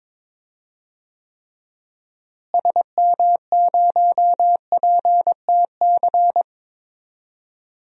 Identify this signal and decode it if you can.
Morse code, heard as a keyed tone, about 22 wpm, 700 Hz. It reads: SM0PTC
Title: DXCC-100 Call Signs - 6 Characters - Repeated 3x - 22wpm